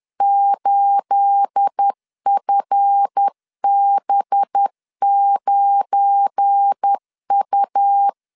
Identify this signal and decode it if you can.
Morse code, heard as a keyed tone, about 10 wpm, 800 Hz. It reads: 8FB9U
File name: CW / DET / 1 9 1500C S B 10wpm 4439